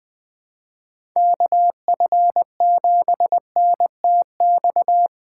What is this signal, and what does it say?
Morse code, heard as a keyed tone, about 20 wpm, 700 Hz. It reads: KF7NTX
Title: USA Call Signs - 6 Characters - 2x Character Spacing - 20wpm